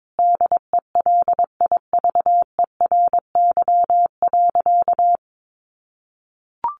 Morse code, heard as a keyed tone, about 22 wpm, 700 Hz. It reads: DELIVERY.
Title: Sentences from Top 1500 Words 22wpm